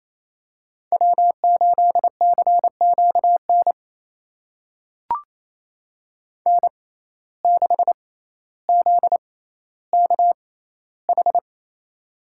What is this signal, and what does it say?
Morse code, heard as a keyed tone, about 28 wpm, spaced as for 14 wpm, 700 Hz. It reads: W8CQN N6ZKH
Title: USA Call Signs - 5 Characters - 6x Character Spacing - 28wpm